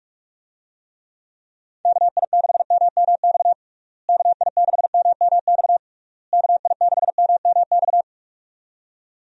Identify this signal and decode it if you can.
Morse code, heard as a keyed tone, about 46 wpm, 700 Hz. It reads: KI6MMX KI6MMX KI6MMX
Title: USA Call Signs - 6 Characters - Repeated 3x - 45wpm